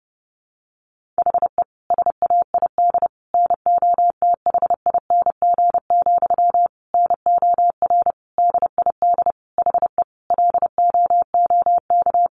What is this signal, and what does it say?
Morse code, heard as a keyed tone, about 28 wpm, 700 Hz. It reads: HE SAID NOTHING, NOR DID HE LOOK
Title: Sentences from Top 1500 Words 30wpm